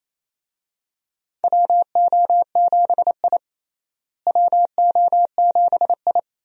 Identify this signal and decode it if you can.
Morse code, heard as a keyed tone, about 28 wpm, 700 Hz. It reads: WO7I WO7I
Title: USA Call Signs - 4 Characters - Repeated 3x - 28wpm